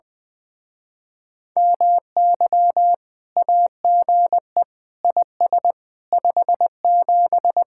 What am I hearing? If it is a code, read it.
Morse code, heard as a keyed tone, about 20 wpm, 700 Hz. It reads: MY AGE IS 57
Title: Speed-Racing Common QSO Phrases - 20wpm